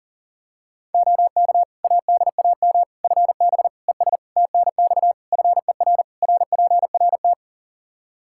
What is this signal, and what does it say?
Morse code, heard as a keyed tone, about 40 wpm, 700 Hz. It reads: OK ADAM FB ES TNX FER RPRT
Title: Practice QSO - Exchange 2 - Info - 40wpm